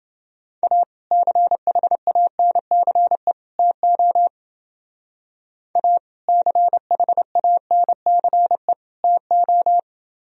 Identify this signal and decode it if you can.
Morse code, heard as a keyed tone, about 28 wpm, 700 Hz. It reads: A CHANCE TO A CHANCE TO